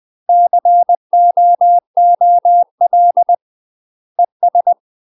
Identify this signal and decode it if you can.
Morse code, heard as a keyed tone, about 20 wpm, 700 Hz. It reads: COOL ES
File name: UK MASTER QSOs TRAINING / 20WPM / UK MASTER QSOs 0061